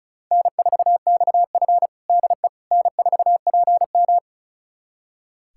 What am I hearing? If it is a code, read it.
Morse code, heard as a keyed tone, about 36 wpm, 700 Hz. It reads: N4XF DE N4PM
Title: USA Callsign DE Callsign - 4 Characters - 35wpm